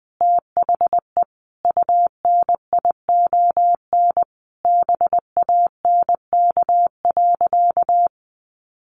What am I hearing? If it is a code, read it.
Morse code, heard as a keyed tone, about 20 wpm, 700 Hz. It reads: THE UNION BANK.
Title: Sentences from Top 1400 Words 20wpm